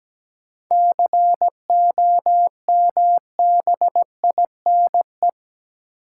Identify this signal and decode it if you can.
Morse code, heard as a keyed tone, about 17 wpm, 700 Hz. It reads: COMBINE
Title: Sentences from Top 1600 Words - New Word Drill Alt - 17wpm